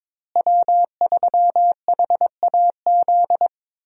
Morse code, heard as a keyed tone, about 22 wpm, 700 Hz. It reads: W3HAZ
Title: USA Call Signs - 5 Characters - Repeated 3x - 22wpm